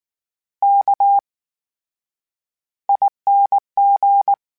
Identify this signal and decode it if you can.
Morse code, heard as a keyed tone, about 19 wpm, spaced as for 9 wpm, 800 Hz. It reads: KING
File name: Special INGs / Special INGs-mp3s / VerySlowToFaster Ing @18